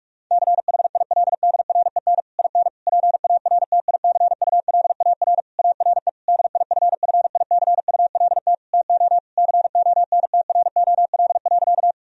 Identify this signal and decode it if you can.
Morse code, heard as a keyed tone, about 47 wpm, 700 Hz. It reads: CHILDREN IN PARTICULAR ARE DIFFICULT TO CONTROL.
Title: Sentences from Top 1000 Words - Review Part 4 - 45wpm